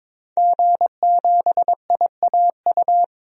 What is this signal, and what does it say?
Morse code, heard as a keyed tone, about 22 wpm, 700 Hz. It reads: G7IAU
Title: DXCC-100 Call Signs - 5 Characters - 22wpm